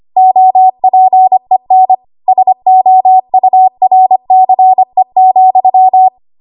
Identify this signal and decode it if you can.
Morse code, heard as a keyed tone, about 25 wpm, 750 Hz. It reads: OPEN SOURCE,
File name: Archive / 25 WPM / 25WPM